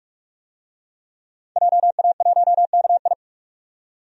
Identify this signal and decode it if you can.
Morse code, heard as a keyed tone, about 46 wpm, 700 Hz. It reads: JA1KI
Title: DXCC-100 Call Signs - 5 Characters - 45wpm